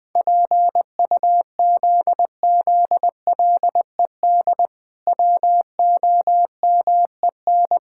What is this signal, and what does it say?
Morse code, heard as a keyed tone, about 20 wpm, 700 Hz. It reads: PUZZLED WOMEN
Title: Pangram Challenge v2 - 20wpm